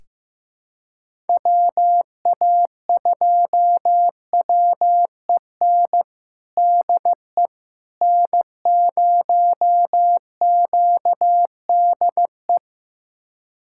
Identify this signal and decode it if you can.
Morse code, heard as a keyed tone, about 14 wpm, 700 Hz. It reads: WA2WEN DE N0QDE
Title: USA Callsign DE Callsign 15wpm